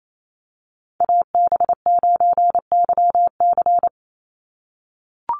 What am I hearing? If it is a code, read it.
Morse code, heard as a keyed tone, about 28 wpm, 700 Hz. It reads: AB9YC
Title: Call Signs - 28wpm